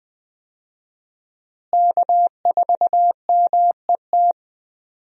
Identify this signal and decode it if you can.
Morse code, heard as a keyed tone, about 20 wpm, 700 Hz. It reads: K4MET
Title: USA Call Signs - 5 Characters - Repeated 2x - 20wpm